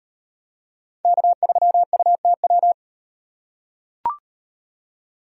Morse code, heard as a keyed tone, about 36 wpm, 700 Hz. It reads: K3UTW